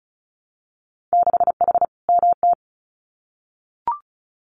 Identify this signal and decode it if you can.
Morse code, heard as a keyed tone, about 36 wpm, 700 Hz. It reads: 6H MT
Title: Winter Field Day Exchanges Encore 35wpm